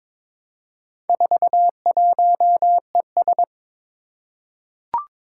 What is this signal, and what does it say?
Morse code, heard as a keyed tone, about 22 wpm, 700 Hz. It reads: 41ES
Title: Four Challenging Letter-Number Combinations 22wpm